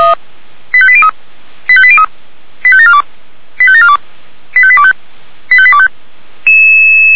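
Five-tone selective call (ZVEI), heard as quick runs of five tones, tones of about 70 ms. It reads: 85803 85803 85742 85742 85825 85825
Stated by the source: Five-tone ZVEI call received on an SDR.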